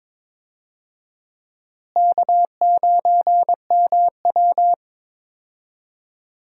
Morse code, heard as a keyed tone, about 21 wpm, 700 Hz. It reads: K9MW
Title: USA Call Signs - 4 Characters - Repeated 2x - 22wpm